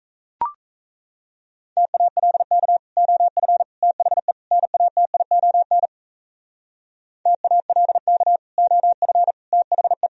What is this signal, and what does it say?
Morse code, heard as a keyed tone, about 34 wpm, 700 Hz. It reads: TALK OF THE NATION TALK OF THE
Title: Speed-Racing - Sets of 4 Words 28wpm